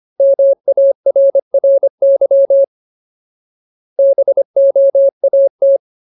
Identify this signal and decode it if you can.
Morse code, heard as a keyed tone, about 25 wpm, 550 Hz. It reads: MARRY BOAT
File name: Ultimate Word List / 25WPM Ultimate